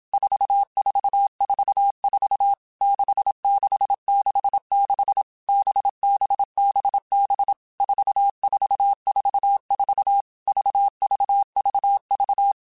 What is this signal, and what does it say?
Morse code, heard as a keyed tone, about 27 wpm, 800 Hz. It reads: 4444 6666 BBBB 4444 VVVV
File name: Basic Letter Groups By7Groups / VB46 / VB46@25